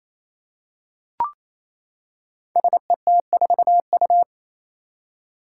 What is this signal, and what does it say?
Morse code, heard as a keyed tone, about 27 wpm, 700 Hz. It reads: SET4U
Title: Five Challenging Letter-Number Combinations 28wpm